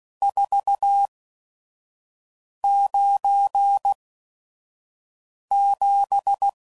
Morse code, heard as a keyed tone, about 16 wpm, spaced as for 5 wpm, 800 Hz. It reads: 497
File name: CW Code Groups @5WPM / RST's / Various RST No's